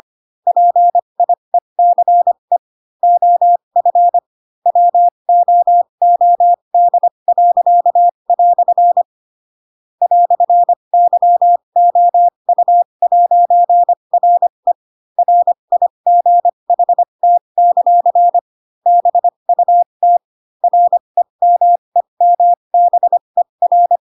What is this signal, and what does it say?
Morse code, heard as a keyed tone, about 25 wpm, spaced as for 22 wpm, 700 Hz. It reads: PIECE OF WOOD." "YOU'RE RIGHT; BUT REMEMBER
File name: Pinocchio / Pinocchio ch2 18 25 ik1hge